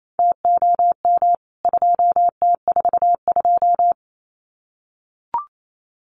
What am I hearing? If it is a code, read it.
Morse code, heard as a keyed tone, about 28 wpm, 700 Hz. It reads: TOM 2T42
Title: CWT Exchange 28wpm